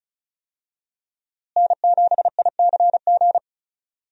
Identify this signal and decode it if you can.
Morse code, heard as a keyed tone, about 34 wpm, 700 Hz. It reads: N7ICG
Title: USA Call Signs - 5 Characters - Repeated 2x - 35wpm